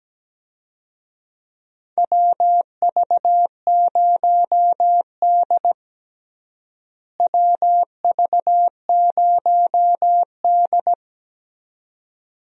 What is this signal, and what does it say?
Morse code, heard as a keyed tone, about 17 wpm, 700 Hz. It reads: WV0D WV0D
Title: USA Call Signs - 4 Characters - Repeated 2x - 17wpm